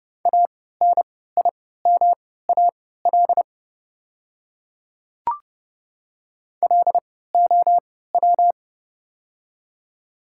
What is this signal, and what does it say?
Morse code, heard as a keyed tone, about 30 wpm, spaced as for 17 wpm, 700 Hz. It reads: ANIMAL LOW
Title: 3x Character Spacing - Top 200 Words - 30wpm